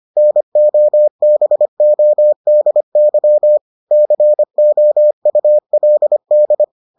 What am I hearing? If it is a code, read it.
Morse code, heard as a keyed tone, about 25 wpm, 600 Hz. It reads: NOBODY COULD